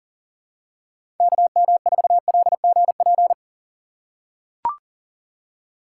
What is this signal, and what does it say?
Morse code, heard as a keyed tone, about 37 wpm, 700 Hz. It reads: KM4LGP
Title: USA Call Signs - 6 Characters - 2x Character Spacing - 40wpm